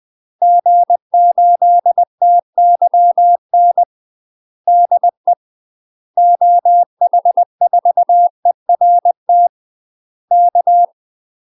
Morse code, heard as a keyed tone, about 20 wpm, 700 Hz. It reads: G8TYN DE OH4ERT K
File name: UK MASTER QSOs TRAINING / 20WPM / UK MASTER QSOs 0049